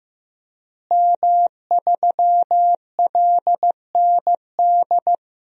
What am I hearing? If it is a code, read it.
Morse code, heard as a keyed tone, about 15 wpm, 700 Hz. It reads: M3LND
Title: DXCC-100 Call Signs - 5 Characters - Repeated 3x - 15wpm